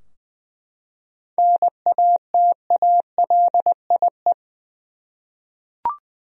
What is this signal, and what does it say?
Morse code, heard as a keyed tone, about 20 wpm, 700 Hz. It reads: NATALIE
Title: USA Common Names - 20wpm